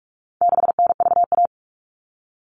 Morse code, heard as a keyed tone, about 44 wpm, 700 Hz. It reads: 6NVA
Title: All Cut Numbers - 4-Digit 45wpm